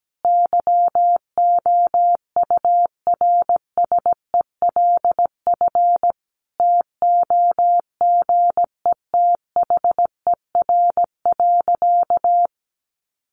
Morse code, heard as a keyed tone, about 17 wpm, 700 Hz. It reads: YOURSELF TOGETHER.
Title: Sentences from Top 600 Words 17wpm